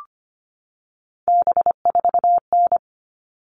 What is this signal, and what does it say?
Morse code, heard as a keyed tone, about 25 wpm, 700 Hz. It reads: B4N
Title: All Cut Numbers - 3-Digit 25wpm